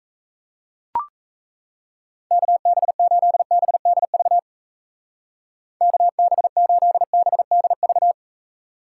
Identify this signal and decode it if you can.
Morse code, heard as a keyed tone, about 36 wpm, 700 Hz. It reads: KB8BDV KB8BDV